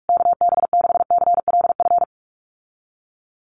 Morse code, heard as a keyed tone, about 44 wpm, 700 Hz. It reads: KB6CLF
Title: USA Call Signs Repeated 3x 45wpm